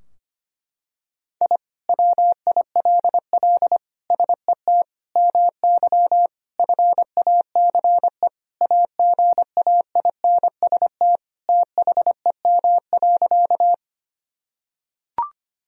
Morse code, heard as a keyed tone, about 25 wpm, 700 Hz. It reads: I WILL SET MY FACE AGAINST THEM.
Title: Sentences from Top 1000 Words - Review Part 2 - 25wpm